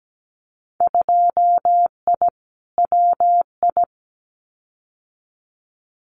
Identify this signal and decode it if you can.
Morse code, heard as a keyed tone, about 17 wpm, 700 Hz. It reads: 2I WI
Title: Winter Field Day Exchanges 17wpm